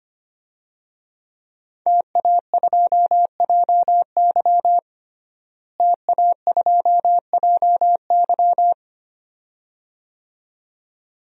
Morse code, heard as a keyed tone, about 25 wpm, 700 Hz. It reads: TA2JY TA2JY
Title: DXCC-100 Call Signs - 5 Characters - Repeated 2x - 25wpm